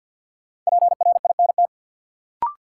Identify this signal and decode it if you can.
Morse code, heard as a keyed tone, about 53 wpm, 700 Hz. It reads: PRINT
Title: Sentences from Top 1400 Words - New Word Drill - 50wpm